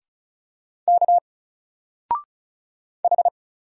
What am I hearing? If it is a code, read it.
Morse code, heard as a keyed tone, about 35 wpm, 700 Hz. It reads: K H
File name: Single-Letter-Number-35wpm